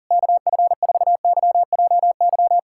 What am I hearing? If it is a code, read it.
Morse code, heard as a keyed tone, about 40 wpm, 700 Hz. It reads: KF4YJY
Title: USA Callsign DE Callsign - 6 Characters - 40wpm